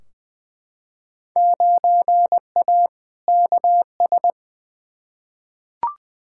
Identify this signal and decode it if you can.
Morse code, heard as a keyed tone, about 21 wpm, 700 Hz. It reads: 9A KS
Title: Field Day Exchanges Encore - 20wpm